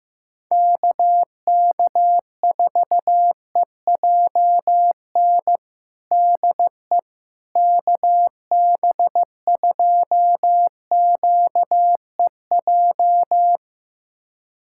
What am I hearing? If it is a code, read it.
Morse code, heard as a keyed tone, about 15 wpm, 700 Hz. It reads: KK4EJN DE KB2QEJ
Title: USA Callsign DE Callsign 15wpm